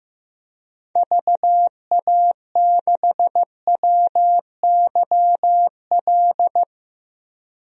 Morse code, heard as a keyed tone, about 15 wpm, 700 Hz. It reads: VA6WYL
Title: DXCC-100 Call Signs - 6 Characters - 15wpm